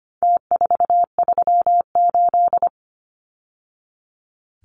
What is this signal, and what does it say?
Morse code, heard as a keyed tone, about 25 wpm, 700 Hz. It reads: T438
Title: Numbers - Contest Style - 4-Digit 25wpm